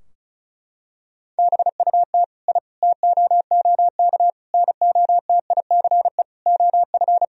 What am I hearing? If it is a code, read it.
Morse code, heard as a keyed tone, about 34 wpm, 700 Hz. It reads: BUT I TOOK NOTICE OF
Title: Sentences from Top 1000 Words - Review Part 2 - 35wpm